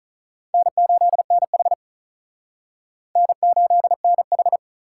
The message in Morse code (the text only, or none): N8NH N8NH